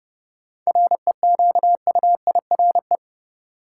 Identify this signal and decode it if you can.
Morse code, heard as a keyed tone, about 31 wpm, 700 Hz. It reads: REQUIRE
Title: Sentences from Top 900 Words - New Word Drill - 30wpm